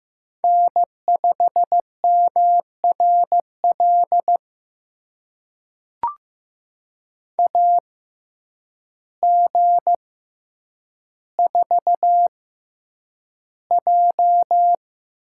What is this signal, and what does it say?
Morse code, heard as a keyed tone, about 15 wpm, spaced as for 8 wpm, 700 Hz. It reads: N5MRL AG4J
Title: USA Call Signs - 5 Characters - 6x Character Spacing - 15wpm